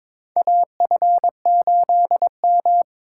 Morse code, heard as a keyed tone, about 22 wpm, 700 Hz. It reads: AF8M
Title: USA Call Signs - 4 Characters - 2x Character Spacing - 22wpm